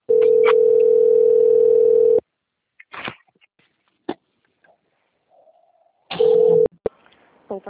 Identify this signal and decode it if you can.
Ringback tone (North America), 440 + 480 Hz.
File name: out- - - - - .5380